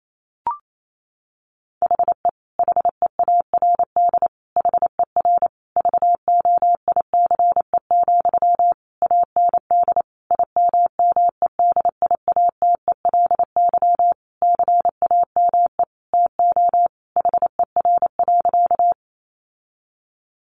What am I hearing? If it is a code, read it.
Morse code, heard as a keyed tone, about 28 wpm, 700 Hz. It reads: HE HEARD HER VOICE, AND IMMEDIATELY CAME TO HER.
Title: Sentences from Top 1100 Words 28wpm